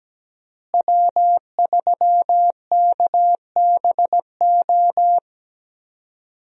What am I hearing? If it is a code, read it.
Morse code, heard as a keyed tone, about 17 wpm, 700 Hz. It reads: W3KBO
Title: USA Call Signs - 5 Characters - Repeated 3x - 17wpm